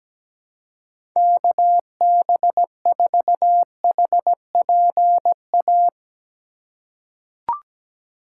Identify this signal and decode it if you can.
Morse code, heard as a keyed tone, about 17 wpm, 700 Hz. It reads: KB4HPA
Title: USA Call Signs - 6 Characters - 2x Character Spacing - 17wpm